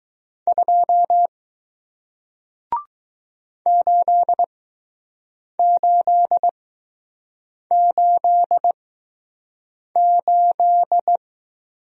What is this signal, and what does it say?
Morse code, heard as a keyed tone, about 20 wpm, spaced as for 6 wpm, 700 Hz. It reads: 2 8888